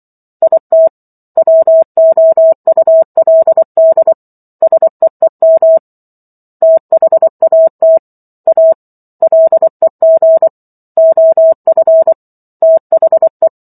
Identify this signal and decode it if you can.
Morse code, heard as a keyed tone, about 24 wpm, 650 Hz. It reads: IT WOULD SEEM THAT A LEG OF THE